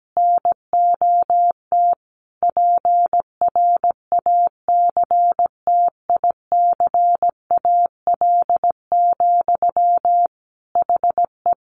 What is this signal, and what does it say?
Morse code, heard as a keyed tone, about 17 wpm, 700 Hz. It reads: NOT PRACTICAL, HE
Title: Sentences from Top 1400 Words 17wpm